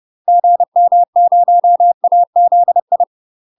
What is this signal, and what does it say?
Morse code, heard as a keyed tone, about 30 wpm, 700 Hz. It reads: GM0AZI
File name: UK MASTER QSOs TRAINING / 30WPM / UK MASTER QSOs 0045